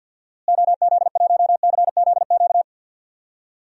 Morse code, heard as a keyed tone, about 52 wpm, 700 Hz. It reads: OZ1CZQ
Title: DXCC-100 Call Signs - 6 Characters - Repeated 2x - 50wpm